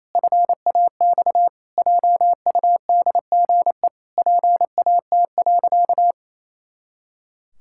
Morse code, heard as a keyed tone, about 27 wpm, 700 Hz. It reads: FAX JUDGE PAT.
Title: Pangram Challenge v2 - 28wpm